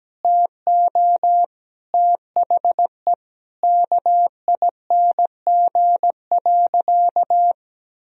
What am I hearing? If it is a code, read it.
Morse code, heard as a keyed tone, about 17 wpm, 700 Hz. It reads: TO THE KING.
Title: Sentences from Top 1000 Words - Review Part 4 - 17wpm